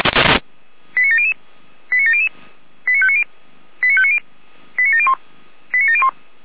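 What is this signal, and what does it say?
Five-tone selective call (ZVEI), heard as quick runs of five tones, tones of about 70 ms. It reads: 89700 89700 89590 89590 89701 89701
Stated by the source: Five-tone ZVEI selective call 89120 received on an SDR.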